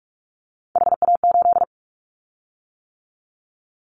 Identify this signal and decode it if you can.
Morse code, heard as a keyed tone, about 44 wpm, 700 Hz. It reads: HA8
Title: Callsign Prefixes for Top 50 DXCC - 45wpm